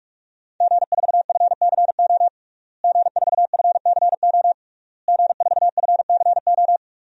Morse code, heard as a keyed tone, about 46 wpm, 700 Hz. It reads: G4FCO G4FCO G4FCO
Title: DXCC-100 Call Signs - 5 Characters - Repeated 3x - 45wpm